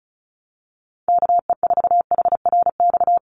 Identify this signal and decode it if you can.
Morse code, heard as a keyed tone, about 34 wpm, 700 Hz. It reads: KE4HRX
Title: USA Call Signs 35wpm